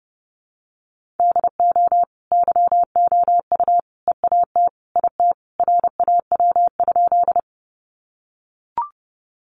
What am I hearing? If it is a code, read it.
Morse code, heard as a keyed tone, about 30 wpm, 700 Hz. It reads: DO YOU EAT IT RAW?
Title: Sentences from Top 1500 Words 30wpm